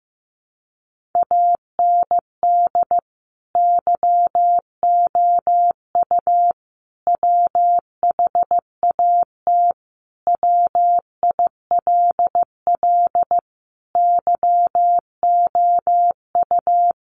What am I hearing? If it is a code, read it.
Morse code, heard as a keyed tone, about 15 wpm, 700 Hz. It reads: AND YOU WHAT WILL YOU
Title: Sentences from Top 600 Words 15wpm